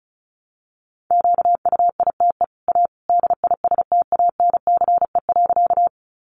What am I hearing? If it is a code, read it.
Morse code, heard as a keyed tone, about 34 wpm, 700 Hz. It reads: QUITE A DISTANCE.
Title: Sentences from Top 1200 Words 35wpm